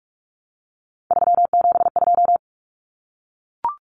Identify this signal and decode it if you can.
Morse code, heard as a keyed tone, about 44 wpm, 700 Hz. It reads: 372
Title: Common Cut Numbers - 3-Digit 45wpm